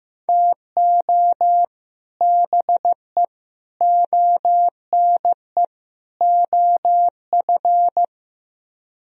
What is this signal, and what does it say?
Morse code, heard as a keyed tone, about 15 wpm, 700 Hz. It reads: TO BE ONE OF
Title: Speed-Racing - Sets of 4 Words 15wpm